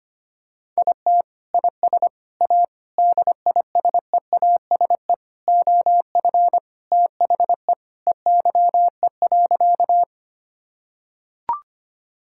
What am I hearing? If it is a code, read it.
Morse code, heard as a keyed tone, about 25 wpm, 700 Hz. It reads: IT IS A DISEASE OF THE EYE.
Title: Sentences from Top 1000 Words - Review Part 6 - 25wpm